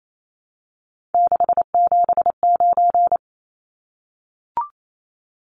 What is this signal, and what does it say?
Morse code, heard as a keyed tone, about 28 wpm, 700 Hz. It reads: 679
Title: Numbers - 3-Digit 28wpm